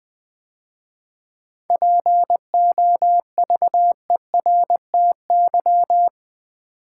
Morse code, heard as a keyed tone, about 20 wpm, 700 Hz. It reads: POVERTY
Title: 7 Letter Words - First 3 Letters Given - 20wpm